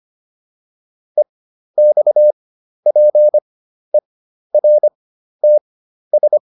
Morse code, heard as a keyed tone, about 25 wpm, spaced as for 12 wpm, 600 Hz. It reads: EXPERTS